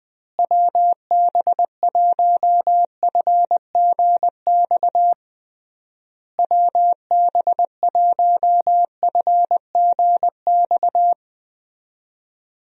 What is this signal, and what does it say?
Morse code, heard as a keyed tone, about 20 wpm, 700 Hz. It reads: WB1FGX WB1FGX
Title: USA Call Signs - 6 Characters - Repeated 3x - 20wpm